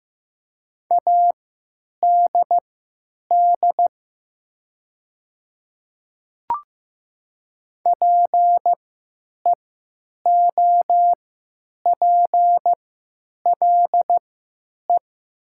Morse code, heard as a keyed tone, about 15 wpm, spaced as for 9 wpm, 700 Hz. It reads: ADD PEOPLE
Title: 3x Character Spacing - Top 200 Words - 15wpm